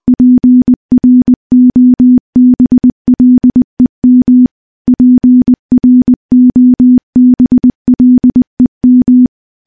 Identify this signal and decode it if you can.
Morse code, heard as a keyed tone, about 20 wpm, 250 Hz. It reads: PROBLEM PROBLEM
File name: morse-training-20WPM